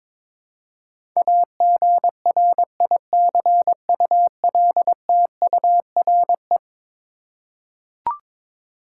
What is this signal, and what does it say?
Morse code, heard as a keyed tone, about 22 wpm, 700 Hz. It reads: AGRICULTURE
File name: practice / 22wpm / Eleven Letter Words - 22wpm